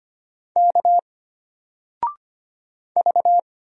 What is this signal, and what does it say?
Morse code, heard as a keyed tone, about 25 wpm, 700 Hz. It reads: K V
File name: Single-Letter-Number-25wpm